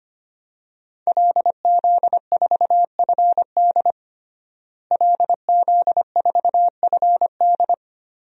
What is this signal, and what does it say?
Morse code, heard as a keyed tone, about 25 wpm, 700 Hz. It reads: LZ4FD LZ4FD
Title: DXCC-100 Call Signs - 5 Characters - Repeated 3x - 25wpm